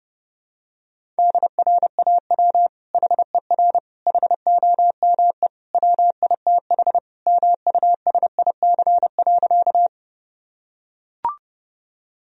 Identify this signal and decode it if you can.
Morse code, heard as a keyed tone, about 30 wpm, 700 Hz. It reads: DRAW HER HOME WITH MUSIC.
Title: Sentences from Top 1000 Words - Review Part 1 - 30wpm